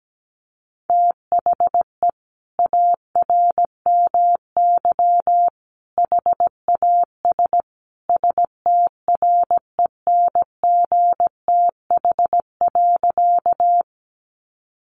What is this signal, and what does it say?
Morse code, heard as a keyed tone, about 17 wpm, 700 Hz. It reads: THE ARMY HAS STRENGTH.
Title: Sentences from Top 1300 Words 17wpm